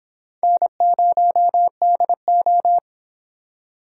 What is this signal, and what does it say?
Morse code, heard as a keyed tone, about 25 wpm, 700 Hz. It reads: N0DO